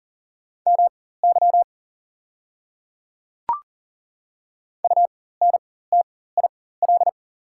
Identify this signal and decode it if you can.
Morse code, heard as a keyed tone, about 37 wpm, spaced as for 18 wpm, 700 Hz. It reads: MY UNTIL
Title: 4x Character Spacing - Top 500 Words - 40wpm